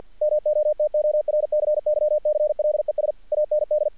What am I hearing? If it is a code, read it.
Morse code, heard as a keyed tone, about 50 wpm, 600 Hz. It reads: MOTORCYCLES AND
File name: archived / CRSnet